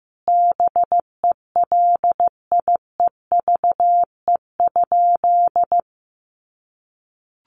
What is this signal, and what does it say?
Morse code, heard as a keyed tone, about 15 wpm, 700 Hz. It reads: BELIEVE?
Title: Sentences from Top 1000 Words 15wpm